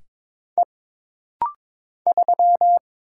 Morse code, heard as a keyed tone, about 22 wpm, 700 Hz. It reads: E 3
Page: This rapid-fire timing is geared to push your ICR (Instant Character Recognition) skills.